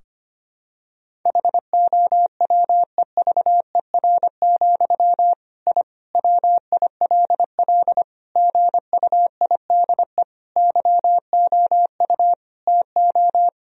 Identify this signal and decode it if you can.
Morse code, heard as a keyed tone, about 25 wpm, 700 Hz. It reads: HOWEVER, I WILL GUIDE YOU TO
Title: Sentences from Top 1000 Words - Review Part 3 - 25wpm